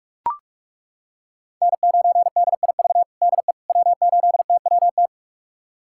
Morse code, heard as a keyed tone, about 41 wpm, 700 Hz. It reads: N9DIV DE W8TWT
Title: USA Callsign DE Callsign - 5 Characters - 45wpm